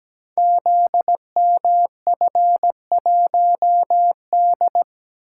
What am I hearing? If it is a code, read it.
Morse code, heard as a keyed tone, about 17 wpm, 700 Hz. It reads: ZMF1D
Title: Code Groups - 5 Letter:Numbers - 17wpm